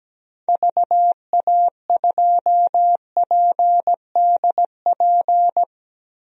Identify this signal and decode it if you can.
Morse code, heard as a keyed tone, about 17 wpm, 700 Hz. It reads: VA2PDP
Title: DXCC-100 Call Signs - 6 Characters - Repeated 2x - 17wpm